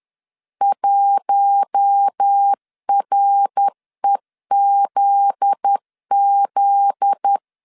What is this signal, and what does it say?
Morse code, heard as a keyed tone, about 10 wpm, 800 Hz. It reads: 1REZZ
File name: LEZIONI / 7 GF9Z / 7 GF9Z 10wpm 2611